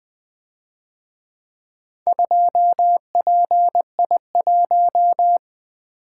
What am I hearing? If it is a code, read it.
Morse code, heard as a keyed tone, about 23 wpm, 700 Hz. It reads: 2PI1
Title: Speed-Racing - Four Letter-Number Combinations - 20wpm